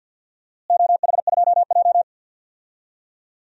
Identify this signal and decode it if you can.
Morse code, heard as a keyed tone, about 53 wpm, 700 Hz. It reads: OH2J
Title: DXCC-100 Call Signs - 4 Characters - Repeated 2x - 50wpm